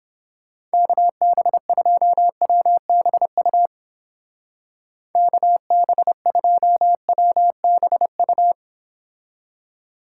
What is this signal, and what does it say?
Morse code, heard as a keyed tone, about 29 wpm, 700 Hz. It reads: KB2WBU KB2WBU